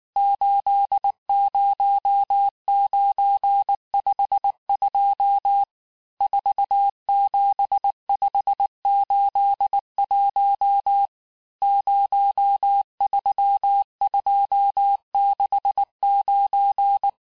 Morse code, heard as a keyed tone, about 19 wpm, 800 Hz. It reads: 80952 47581 03269
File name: cours-du-lundi files / cours-1589-texte-chiffre-vitesse-18mpm